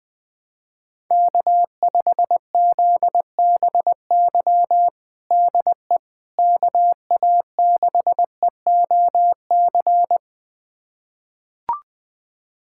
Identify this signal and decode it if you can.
Morse code, heard as a keyed tone, about 20 wpm, 700 Hz. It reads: K5ZBY DE KA6EOC
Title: USA Callsign DE Callsign 20wpm